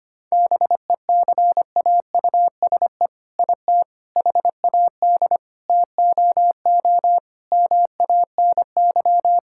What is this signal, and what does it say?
Morse code, heard as a keyed tone, about 25 wpm, 700 Hz. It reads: BECAUSE IT HAD TOO MANY
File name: practice / 25wpm / All-Age Appropriate Jokes - 25wpm